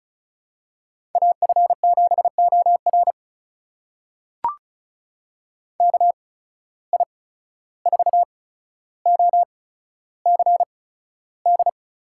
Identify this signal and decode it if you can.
Morse code, heard as a keyed tone, about 36 wpm, spaced as for 14 wpm, 700 Hz. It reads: AF7OR KI4OCD
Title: USA Call Signs - 8x Character Spacing - 35wpm